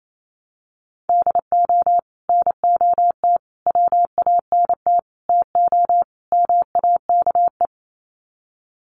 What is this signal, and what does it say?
Morse code, heard as a keyed tone, about 28 wpm, 690 Hz. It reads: DO NOT WANT TO MAKE
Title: Sets of 5 Words Encore 28wpm